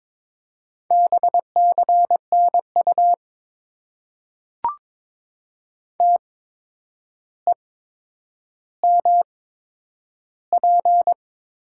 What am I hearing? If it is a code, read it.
Morse code, heard as a keyed tone, about 22 wpm, spaced as for 9 wpm, 700 Hz. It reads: BCNU TEMP
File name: Top 90 QSO Elements - 8x Character Spacing - 22wpm